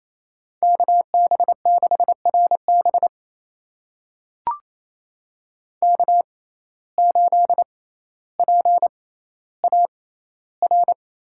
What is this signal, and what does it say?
Morse code, heard as a keyed tone, about 28 wpm, spaced as for 14 wpm, 700 Hz. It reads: KB6RB K8PAR
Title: USA Call Signs - 5 Characters - 6x Character Spacing - 28wpm